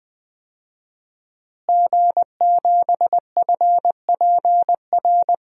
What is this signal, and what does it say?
Morse code, heard as a keyed tone, about 20 wpm, 700 Hz. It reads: G7FPR
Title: DXCC-100 Call Signs - 5 Characters - Repeated 2x - 20wpm